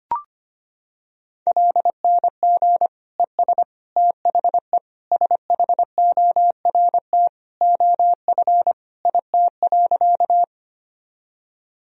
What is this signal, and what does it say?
Morse code, heard as a keyed tone, about 25 wpm, 700 Hz. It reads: LNG ES THE SHORT OF IT.
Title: Sentences from Top 400 Words and Abbreviations 25wpm